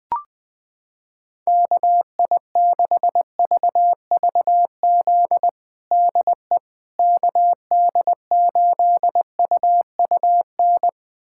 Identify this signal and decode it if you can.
Morse code, heard as a keyed tone, about 20 wpm, 700 Hz. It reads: KI6VVZ DE KD8UUN
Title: USA Callsign DE Callsign - 6 Characters - 20wpm